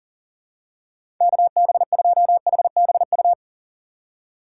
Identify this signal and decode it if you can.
Morse code, heard as a keyed tone, about 40 wpm, 700 Hz. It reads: KB2HBU
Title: USA Call Signs - 6 Characters - 2x Character Spacing - 40wpm